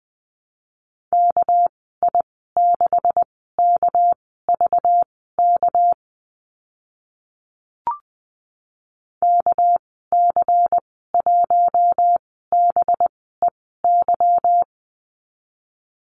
Morse code, heard as a keyed tone, about 20 wpm, spaced as for 14 wpm, 700 Hz. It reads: KI6KVK KC1BEY
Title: 200 Lesson 32g - Letter k - Call Signs 20wpm